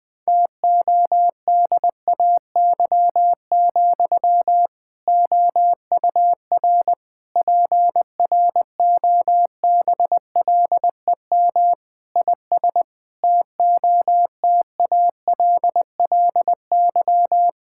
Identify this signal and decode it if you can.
Morse code, heard as a keyed tone, about 21 wpm, 700 Hz. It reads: TODAY, OUR PROBLEM IS TOTALLY
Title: Sentences from Top 1600 Words 20wpm